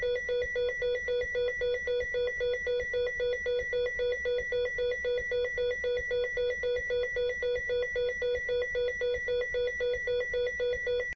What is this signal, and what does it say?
There is no code.